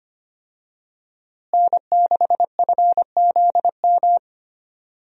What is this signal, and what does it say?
Morse code, heard as a keyed tone, about 25 wpm, 700 Hz. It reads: N6FZM
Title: USA Call Signs - 5 Characters - Repeated 3x - 25wpm